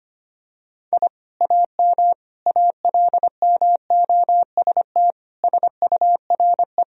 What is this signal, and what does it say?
Morse code, heard as a keyed tone, about 25 wpm, 700 Hz. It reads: I AM ALMOST SURE
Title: Sentences from Top 1000 Words - Review Part 4 - 25wpm